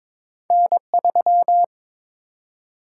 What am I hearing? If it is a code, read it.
Morse code, heard as a keyed tone, about 22 wpm, 700 Hz. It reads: N3
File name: practice / 22wpm / Two Letter-Number Combinations - 22wpm